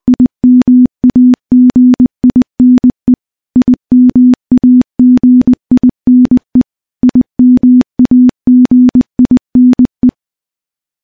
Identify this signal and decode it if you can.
Morse code, heard as a keyed tone, about 20 wpm, 250 Hz. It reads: IMAGINE IMAGINE IMAGINE
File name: morse / morse-training-20WPM